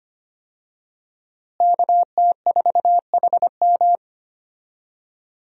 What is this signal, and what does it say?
Morse code, heard as a keyed tone, about 25 wpm, 700 Hz. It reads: KT4HM
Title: USA Call Signs - 5 Characters - Repeated 2x - 25wpm